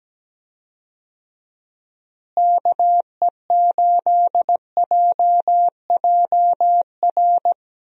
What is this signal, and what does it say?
Morse code, heard as a keyed tone, about 17 wpm, 700 Hz. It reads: KE8JJR
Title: USA Call Signs - 6 Characters - Repeated 3x - 17wpm